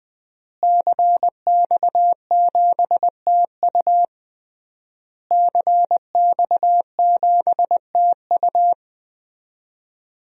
Morse code, heard as a keyed tone, about 21 wpm, 700 Hz. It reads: CX7TU CX7TU
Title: DXCC-100 Call Signs - 5 Characters - Repeated 3x - 20wpm